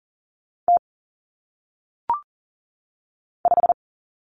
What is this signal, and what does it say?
Morse code, heard as a keyed tone, about 40 wpm, 700 Hz. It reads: T 5
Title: Common Cut Numbers - 1-Digit 40wpm